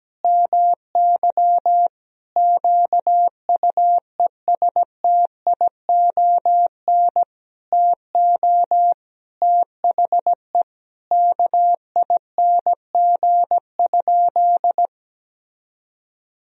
Morse code, heard as a keyed tone, about 17 wpm, 700 Hz. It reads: MY QUESTION TO THE KING?
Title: Sentences from Top 1000 Words - Review Part 2 - 17wpm